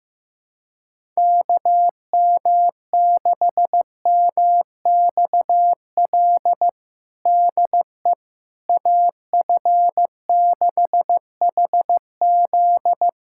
Morse code, heard as a keyed tone, about 15 wpm, 700 Hz. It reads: KM6MXL DE AF6HZ
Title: USA Callsign DE Callsign 15wpm